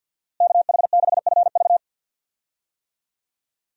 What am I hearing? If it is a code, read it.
Morse code, heard as a keyed tone, about 48 wpm, 700 Hz. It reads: KH6FV
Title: USA Call Signs - 5 Characters - Repeated 2x - 50wpm